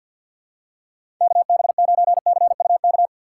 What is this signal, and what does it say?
Morse code, heard as a keyed tone, about 52 wpm, 690 Hz. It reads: KB9CUK
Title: USA Call Signs - 6 Characters - 3x Character Spacing - 50wpm